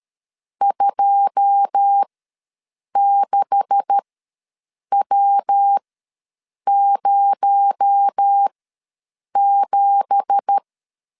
Morse code, heard as a keyed tone, about 13 wpm, spaced as for 7 wpm, 800 Hz. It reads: 26W07
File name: LEZIONI / 9 2867 ws 12wpm 3431